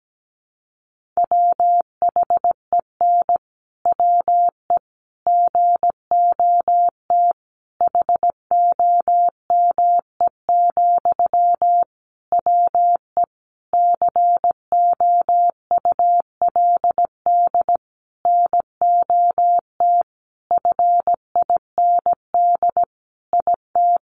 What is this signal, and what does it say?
Morse code, heard as a keyed tone, about 17 wpm, 700 Hz. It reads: WHEN WE GOT HOME, WE COULD NOT FIND IT
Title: Sentences from Top 1400 Words 17wpm